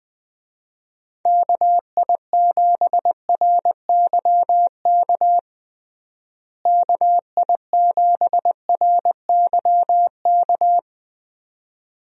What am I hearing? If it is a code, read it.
Morse code, heard as a keyed tone, about 21 wpm, 700 Hz. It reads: KI7RYK KI7RYK
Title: USA Call Signs - 6 Characters - Repeated 2x - 20wpm